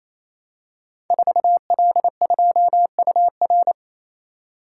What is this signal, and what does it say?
Morse code, heard as a keyed tone, about 28 wpm, 700 Hz. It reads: 4L2UR
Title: Code Groups - 5 Letter:Numbers - 28wpm